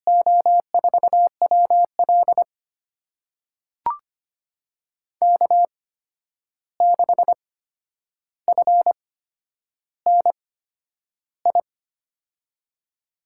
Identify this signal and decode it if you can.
Morse code, heard as a keyed tone, about 25 wpm, spaced as for 10 wpm, 700 Hz. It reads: O4WL K6FNI
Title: USA Call Signs - 5 Characters - 8x Character Spacing - 25wpm